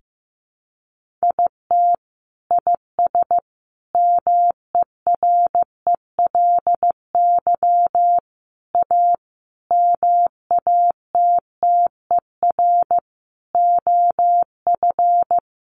Morse code, heard as a keyed tone, about 14 wpm, 700 Hz. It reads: IT IS MERELY A MATTER OF
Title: Sentences from Top 1400 Words 15wpm